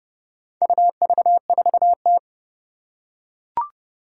Morse code, heard as a keyed tone, about 30 wpm, 700 Hz. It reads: UV4T
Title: Four Challenging Letter-Number Combinations 30wpm